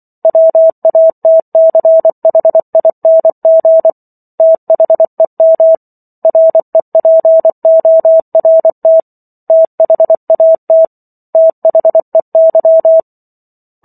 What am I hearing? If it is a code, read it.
Morse code, heard as a keyed tone, about 24 wpm, 650 Hz. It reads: WATCHING THEM REPORT THAT THEY
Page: War of the Worlds - 15-Chapter 15 - 24 WPM